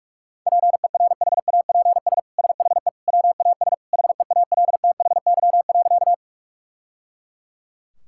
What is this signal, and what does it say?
Morse code, heard as a keyed tone, about 41 wpm, 690 Hz. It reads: PERHAPS SHE WAS HEALTHY.
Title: Sentences from Top 1000 Words - Review Part 5 - 45wpm